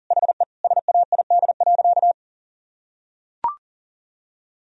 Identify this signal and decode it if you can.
Morse code, heard as a keyed tone, about 40 wpm, 700 Hz. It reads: HE SAID.
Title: Sentences from Top 1000 Words - Review Part 1 - 40wpm